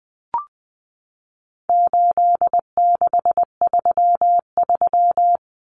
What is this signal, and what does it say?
Morse code, heard as a keyed tone, about 19 wpm, 700 Hz. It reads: 8633
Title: Numbers - 4-Digit 20wpm